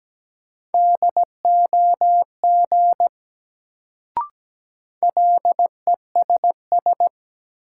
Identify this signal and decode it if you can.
Morse code, heard as a keyed tone, about 17 wpm, 700 Hz. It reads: DOG LESS
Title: Sentences from Top 400 Words - New Word Drill - 17wpm